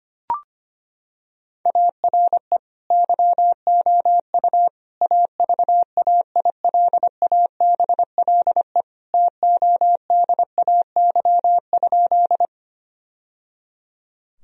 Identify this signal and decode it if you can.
Morse code, heard as a keyed tone, about 25 wpm, 700 Hz. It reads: ARE YOU AVAILABLE TODAY?
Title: Sentences from Top 1000 Words - Review Part 3 - 25wpm